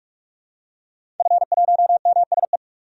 Morse code, heard as a keyed tone, about 46 wpm, 700 Hz. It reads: F1MSE
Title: DXCC-100 Call Signs - 5 Characters - 45wpm